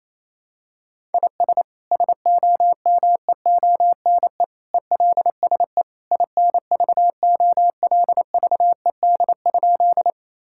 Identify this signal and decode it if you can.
Morse code, heard as a keyed tone, about 28 wpm, 700 Hz. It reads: IS SOMEONE ELSE INVOLVED?
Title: Sentences from Top 1000 Words - Review Part 5 - 28wpm